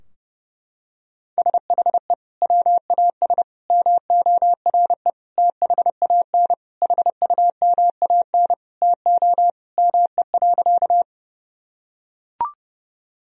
Morse code, heard as a keyed tone, about 30 wpm, 700 Hz. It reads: SHE WAS MORE THAN HUMAN TO ME.
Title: Sentences from Top 1000 Words - Review Part 3 - 30wpm